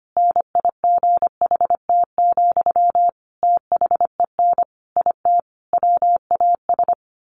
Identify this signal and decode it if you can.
Morse code, heard as a keyed tone, about 25 wpm, 700 Hz. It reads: NIGHT, THEN IT WAS
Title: A Sound Of Thunder 25wpm